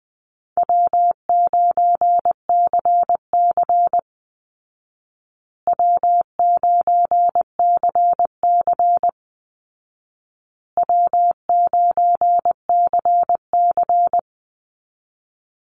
Morse code, heard as a keyed tone, about 20 wpm, 700 Hz. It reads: W9CC W9CC W9CC
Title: Call Signs Repeated 2x - 20wpm